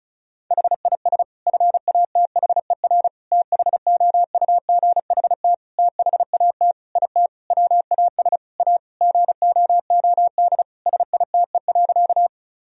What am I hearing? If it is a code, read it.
Morse code, heard as a keyed tone, about 36 wpm, 700 Hz. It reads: HIS FATHER THOUGHT THAT IT WAS A GOOD SITE.
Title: Sentences from Top 1000 Words - Review Part 4 - 35wpm